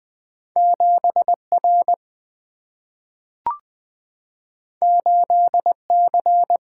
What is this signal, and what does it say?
Morse code, heard as a keyed tone, about 20 wpm, 700 Hz. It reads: 7 R 8 C
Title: Single-Letter Single-Number 20wpm